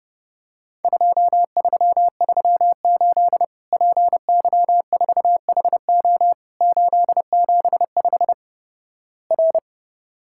Morse code, heard as a keyed tone, about 30 wpm, 700 Hz. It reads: 2338 PY4HO 875 R
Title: Worked All Europe DX Contest - QTC Exchange - 30wpm